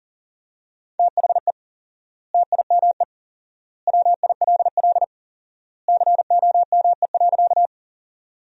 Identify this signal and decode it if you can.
Morse code, heard as a keyed tone, about 40 wpm, 700 Hz. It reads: THE TIME WILL COME.
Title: 4x Word-Spacing - Sentences from Top 100 Words - 40wpm